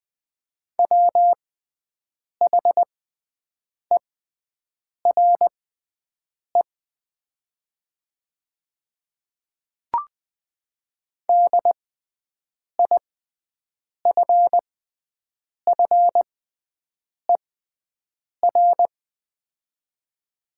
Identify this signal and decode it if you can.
Morse code, heard as a keyed tone, about 20 wpm, spaced as for 7 wpm, 700 Hz. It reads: WHERE DIFFER
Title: 6x Character Spacing - Top 500 Words - 20wpm